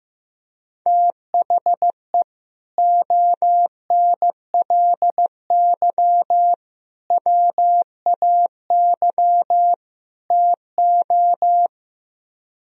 Morse code, heard as a keyed tone, about 15 wpm, 700 Hz. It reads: THE ONLY WAY TO
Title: Speed-Racing - Sets of 4 Words 15wpm